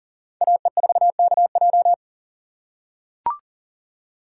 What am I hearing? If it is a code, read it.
Morse code, heard as a keyed tone, about 40 wpm, 700 Hz. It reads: AE4KJ
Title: USA Call Signs - 5 Characters - 6x Character Spacing - 40wpm